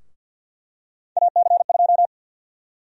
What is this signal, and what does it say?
Morse code, heard as a keyed tone, about 47 wpm, 700 Hz. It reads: AC2
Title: USA Call Sign Prefixes - 50wpm